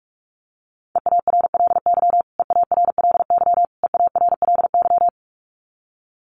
Morse code, heard as a keyed tone, about 44 wpm, 700 Hz. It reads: EARLY EARLY EARLY
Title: Top 500 Words Repeated 45wpm